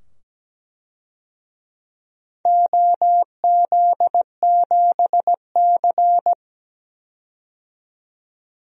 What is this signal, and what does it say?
Morse code, heard as a keyed tone, about 17 wpm, 700 Hz. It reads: OZ7C
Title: DXCC-100 Call Signs - 4 Characters - 17wpm